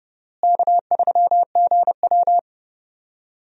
Morse code, heard as a keyed tone, about 30 wpm, 700 Hz. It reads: K3GW
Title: USA Call Signs - 4 Characters - Repeated 3x - 30wpm